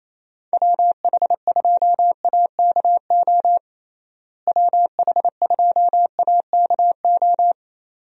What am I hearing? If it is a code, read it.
Morse code, heard as a keyed tone, about 28 wpm, 700 Hz. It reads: WH2AKO WH2AKO
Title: USA Call Signs - 6 Characters - Repeated 2x - 28wpm